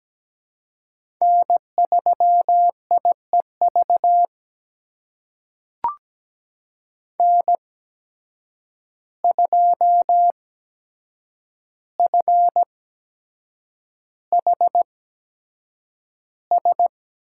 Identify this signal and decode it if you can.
Morse code, heard as a keyed tone, about 17 wpm, spaced as for 7 wpm, 700 Hz. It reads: N3IEV N2FHS
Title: USA Call Signs - 5 Characters - 8x Character Spacing - 17wpm